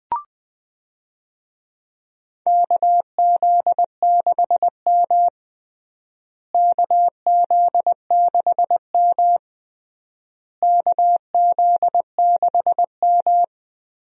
Morse code, heard as a keyed tone, about 21 wpm, 700 Hz. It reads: KZ6M KZ6M KZ6M
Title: USA Call Signs - 4 Characters - Repeated 3x - 20wpm